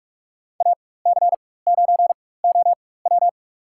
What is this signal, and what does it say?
Morse code, heard as a keyed tone, about 46 wpm, spaced as for 21 wpm, 690 Hz. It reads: AC9OW
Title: USA Call Signs - 5 Characters - 4x Character Spacing - 45wpm